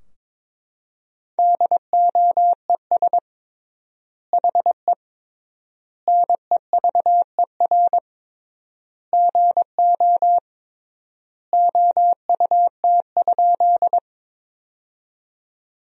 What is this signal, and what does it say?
Morse code, heard as a keyed tone, about 22 wpm, 700 Hz. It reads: DOES HE NEVER GO OUT?
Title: 3x Word-Spacing - Sentences from Top 300 Words - 22wpm